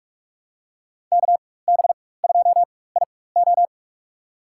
Morse code, heard as a keyed tone, about 47 wpm, spaced as for 21 wpm, 700 Hz. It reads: KB2IO
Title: USA Call Signs - 5 Characters - 4x Character Spacing - 45wpm